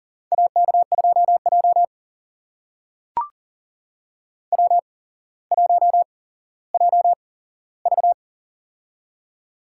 Morse code, heard as a keyed tone, about 42 wpm, spaced as for 17 wpm, 700 Hz. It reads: AK2J W1JV
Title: USA Call Signs - 4 Characters - 8x Character Spacing - 40wpm